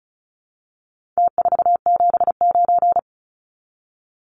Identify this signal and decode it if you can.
Morse code, heard as a keyed tone, about 34 wpm, 700 Hz. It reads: T479
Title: Numbers - Contest Style - 4-Digit 35wpm